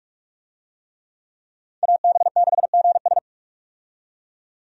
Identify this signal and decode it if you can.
Morse code, heard as a keyed tone, about 46 wpm, 700 Hz. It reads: AB6GS
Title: USA Call Signs - 5 Characters - 45wpm